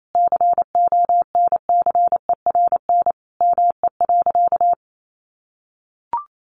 Morse code, heard as a keyed tone, about 28 wpm, 700 Hz. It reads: CONCERN ME.
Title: Sentences from Top 1000 Words 28wpm